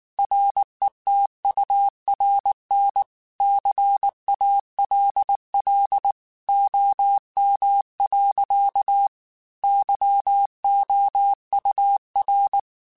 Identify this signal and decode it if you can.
Morse code, heard as a keyed tone, about 19 wpm, 800 Hz. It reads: RETURN CALL OM. YOUR
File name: QSOs / QSOs-mp3s / kc7sac@18